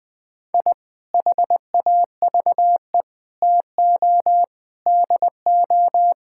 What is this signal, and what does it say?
Morse code, heard as a keyed tone, about 20 wpm, 700 Hz. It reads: I HAVE TO DO
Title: Sentences with 3-Word Phrases - 20wpm